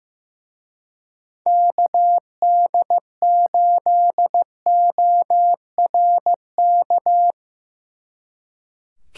Morse code, heard as a keyed tone, about 17 wpm, 700 Hz. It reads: KD8ORK
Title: Speed Racing - USA Call Signs 15wpm